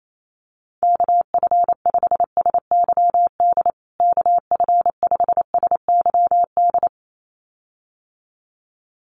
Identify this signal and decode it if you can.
Morse code, heard as a keyed tone, about 28 wpm, 700 Hz. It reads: KF5SYD KF5SYD
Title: USA Call Signs Repeated 2x 28wpm